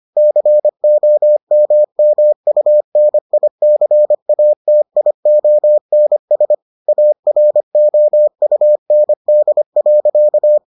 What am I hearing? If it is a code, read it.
Morse code, heard as a keyed tone, about 25 wpm, 600 Hz. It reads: COMMUNICATIONS AROUND.